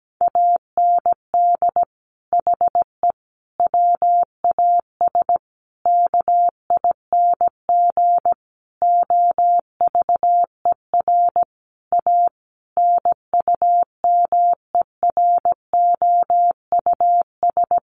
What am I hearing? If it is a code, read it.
Morse code, heard as a keyed tone, about 17 wpm, 700 Hz. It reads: AND HE WAS KING OVER A NUMEROUS
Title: Sentences from Top 1400 Words 17wpm